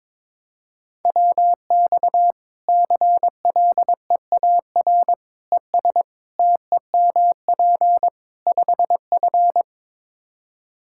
Morse code, heard as a keyed tone, about 22 wpm, 700 Hz. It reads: WX CLEAR ES TEMP 5F
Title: Practice QSO - Exchange 2 - Info - 22wpm